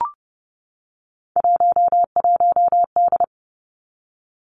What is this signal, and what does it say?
Morse code, heard as a keyed tone, about 30 wpm, 700 Hz. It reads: 11D
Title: Three Challenging Letter-Number Combinations 30wpm